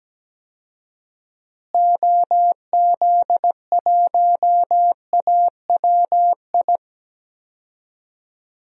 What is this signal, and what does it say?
Morse code, heard as a keyed tone, about 17 wpm, 700 Hz. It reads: OZ1AWI
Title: DXCC-100 Call Signs - 6 Characters - Repeated 3x - 17wpm